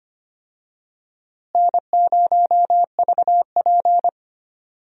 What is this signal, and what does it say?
Morse code, heard as a keyed tone, about 25 wpm, 700 Hz. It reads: N0VP
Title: USA Call Signs - 4 Characters - Repeated 2x - 25wpm